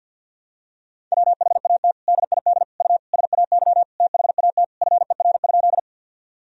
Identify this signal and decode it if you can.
Morse code, heard as a keyed tone, about 48 wpm, 700 Hz. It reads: WHAT DID U SAY THAT FER?
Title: Sentences from Top 200 Words and Abbreviations 50wpm